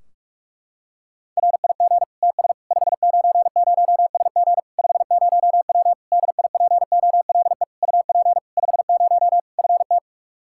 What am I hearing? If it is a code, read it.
Morse code, heard as a keyed tone, about 46 wpm, 700 Hz. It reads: RIG TS 590SG 50W DIPOLE UP 50 FT
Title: Practice QSO - Exchange 2 - Info - 45wpm